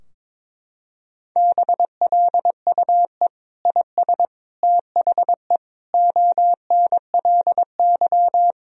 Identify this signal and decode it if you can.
Morse code, heard as a keyed tone, about 22 wpm, 700 Hz. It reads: BLUE IS THE ONLY
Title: Sentences from Top 1000 Words - Review Part 3 - 22wpm